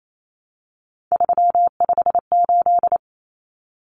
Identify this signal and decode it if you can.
Morse code, heard as a keyed tone, about 28 wpm, 700 Hz. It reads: 358
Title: Common Cut Numbers - 3-Digit 28wpm